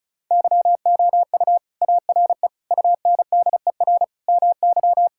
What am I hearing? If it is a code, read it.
Morse code, heard as a keyed tone, about 34 wpm, 700 Hz. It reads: YOU ARE UNDER MY
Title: Sentences from Top 1600 Words 35wpm